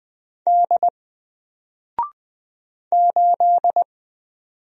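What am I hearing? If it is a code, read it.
Morse code, heard as a keyed tone, about 20 wpm, 700 Hz. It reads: D 8
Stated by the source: Single letter and number Morse drill at 20 wpm, from N4LCD.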